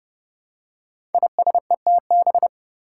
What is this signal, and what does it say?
Morse code, heard as a keyed tone, about 30 wpm, 700 Hz. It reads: ISETB
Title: Five Challenging Letter-Number Combinations 30wpm